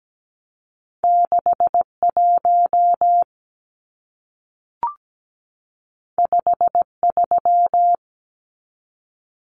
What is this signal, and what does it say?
Morse code, heard as a keyed tone, about 17 wpm, 700 Hz. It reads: 61 53
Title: Numbers - 2-Digit 17wpm